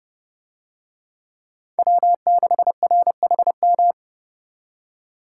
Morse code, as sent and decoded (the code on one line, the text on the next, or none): .-- -.... .-. .... --
W6RHM